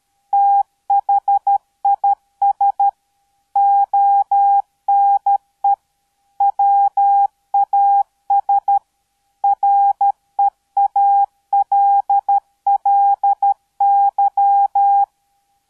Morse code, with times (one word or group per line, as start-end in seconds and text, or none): 0.3-2.9 THIS
3.6-5.7 ONE
6.4-8.8 WAS
9.4-15.0 REALLY